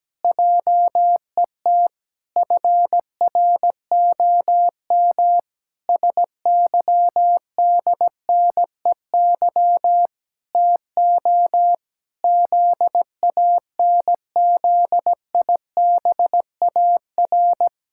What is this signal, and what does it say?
Morse code, heard as a keyed tone, about 17 wpm, 700 Hz. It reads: JET FROM SYDNEY TO ZANZIBAR
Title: Pangram Challenge v2 - 17wpm